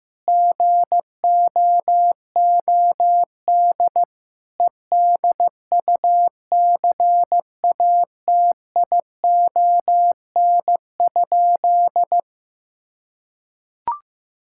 Morse code, heard as a keyed tone, about 15 wpm, 700 Hz. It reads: GOOD EDUCATION?
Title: Sentences from Top 1000 Words - Review Part 6 - 15wpm